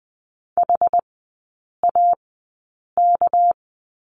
Morse code, heard as a keyed tone, about 20 wpm, spaced as for 8 wpm, 700 Hz. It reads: HAK
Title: 198 Lesson 32e - Letter k - Three Characters 20wpm